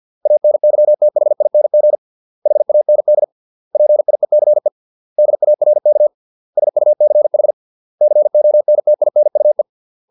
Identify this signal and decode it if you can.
Morse code, heard as a keyed tone, about 50 wpm, 600 Hz. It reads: ANYTHING HAND PIECE DARK SUCH CONTINUE